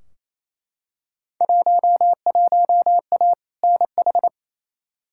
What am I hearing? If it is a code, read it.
Morse code, heard as a keyed tone, about 28 wpm, 700 Hz. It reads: 11A NH
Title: Field Day Exchanges Encore - 28wpm